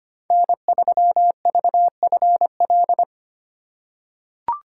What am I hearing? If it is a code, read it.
Morse code, heard as a keyed tone, about 25 wpm, 700 Hz. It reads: N3VFL
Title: USA Call Signs - 2x Character Spacing - 25wpm